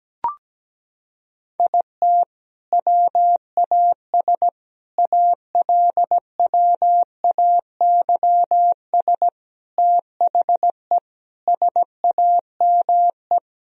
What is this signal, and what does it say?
Morse code, heard as a keyed tone, about 17 wpm, 700 Hz. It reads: IT WAS ALWAYS THE SAME
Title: Sentences from Top 1000 Words - Review Part 1 - 17wpm